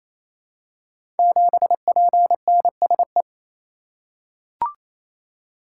Morse code, heard as a keyed tone, about 27 wpm, 700 Hz. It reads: 7PNSE
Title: Code Groups - 5 Letter:Numbers - 28wpm